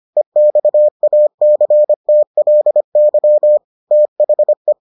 Morse code, heard as a keyed tone, about 25 wpm, 600 Hz. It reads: EXACTLY THE